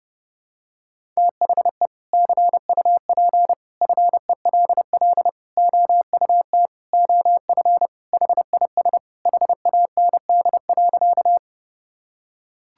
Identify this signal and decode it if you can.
Morse code, heard as a keyed tone, about 30 wpm, 700 Hz. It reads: THE CUP FELL OUT OF HIS HAND.
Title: Sentences from Top 1600 Words 30wpm